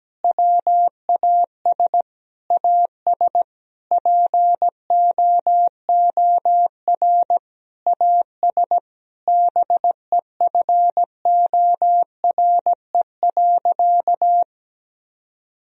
Morse code, heard as a keyed tone, about 17 wpm, 700 Hz. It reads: WAS AS POOR AS BEFORE.
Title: Sentences from Top 1000 Words - Review Part 2 - 17wpm